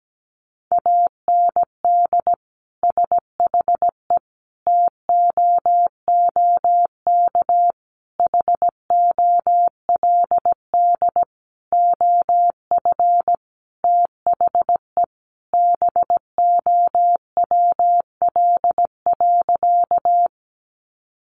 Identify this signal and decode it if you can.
Morse code, heard as a keyed tone, about 17 wpm, 700 Hz. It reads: AND SHE TOOK HOLD OF THE BOWL.
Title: Sentences from Top 1200 Words 17wpm